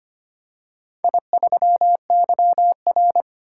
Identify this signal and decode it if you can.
Morse code, heard as a keyed tone, about 25 wpm, 700 Hz. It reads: I3YR
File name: practice / 25wpm / Four Letter-Number Combinations - 25wpm